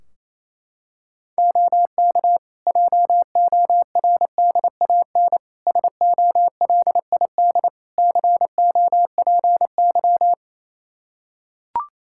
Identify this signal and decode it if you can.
Morse code, heard as a keyed tone, about 27 wpm, 700 Hz. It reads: OK JORDAN SOLID COPY
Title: Practice QSO - Exchange 3 - More Info - 28wpm